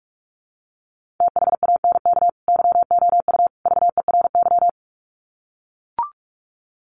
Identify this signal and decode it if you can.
Morse code, heard as a keyed tone, about 44 wpm, 700 Hz. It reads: THANK YOU VERY
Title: Sets of 3 Words 45wpm